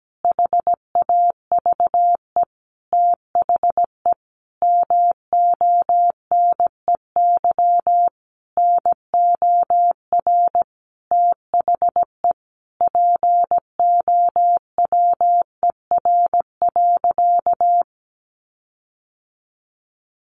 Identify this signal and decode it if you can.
Morse code, heard as a keyed tone, about 17 wpm, 700 Hz. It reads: HAVE THE MONEY NOR THE POWER.
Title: Sentences from Top 1500 Words 17wpm